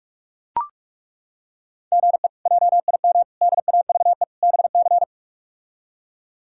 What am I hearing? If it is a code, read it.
Morse code, heard as a keyed tone, about 43 wpm, 700 Hz. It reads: GE JIM DAVE BC
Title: K1USN SST QSO Practice - 45wpm